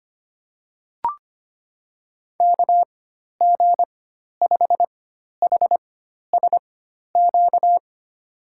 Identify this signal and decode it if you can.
Morse code, heard as a keyed tone, about 25 wpm, spaced as for 12 wpm, 700 Hz. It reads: KG5HSQ
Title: USA Call Signs - 4x Character Spacing - 25wpm